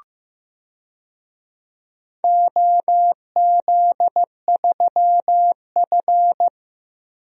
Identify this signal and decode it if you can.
Morse code, heard as a keyed tone, about 15 wpm, 700 Hz. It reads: OZ3F
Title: DXCC-100 Call Signs - 4 Characters - Repeated 3x - 15wpm